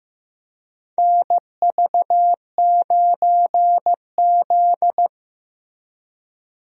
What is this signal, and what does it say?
Morse code, heard as a keyed tone, about 15 wpm, 700 Hz. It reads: NV9Z
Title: USA Call Signs - 4 Characters - Repeated 2x - 15wpm